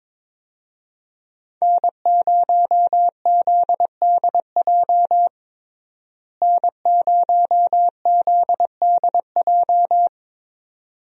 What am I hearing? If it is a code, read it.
Morse code, heard as a keyed tone, about 22 wpm, 700 Hz. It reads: N0ZDJ N0ZDJ
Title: USA Call Signs - 5 Characters - Repeated 2x - 22wpm